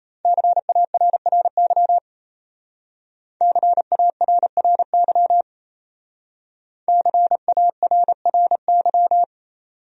Morse code, heard as a keyed tone, about 33 wpm, 700 Hz. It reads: CARRY CARRY CARRY
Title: Speed-Racing Top 300 Words 28wpm